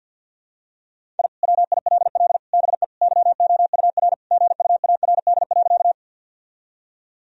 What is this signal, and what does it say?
Morse code, heard as a keyed tone, about 50 wpm, 700 Hz. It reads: I WILL BE YOUR GUARD.
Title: Sentences from Top 1000 Words - Review Part 4 - 50wpm